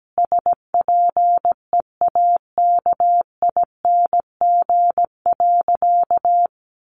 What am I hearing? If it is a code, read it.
Morse code, heard as a keyed tone, about 17 wpm, 700 Hz. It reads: SPEAKING.
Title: A Sound Of Thunder 17wpm